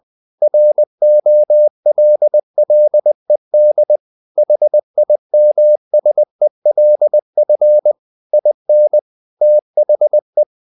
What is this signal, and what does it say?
Morse code, heard as a keyed tone, about 20 wpm, 600 Hz. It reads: ROLLED HIMSELF IN THE
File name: the old man and the sea / the old man and the sea part 1 20wpm